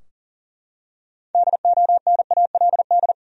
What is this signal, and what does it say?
Morse code, heard as a keyed tone, about 40 wpm, 700 Hz. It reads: DONALD
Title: USA Common Names - 40wpm